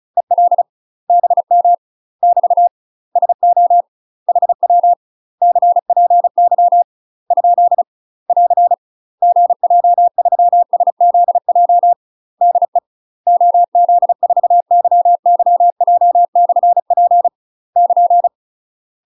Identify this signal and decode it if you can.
Morse code, heard as a keyed tone, about 35 wpm, 700 Hz. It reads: EL BM <BT> SO HW CPY ? <AR> GJ3SZJ DE OZ4YYJ/P <KN>
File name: UK MASTER QSOs TRAINING / 35WPM / UK MASTER QSOs 0016